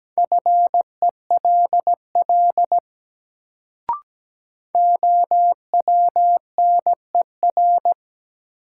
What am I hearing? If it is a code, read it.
Morse code, heard as a keyed tone, about 17 wpm, 700 Hz. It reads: FELL OWNER
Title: Sentences from Top 1600 Words - New Word Drill Alt - 17wpm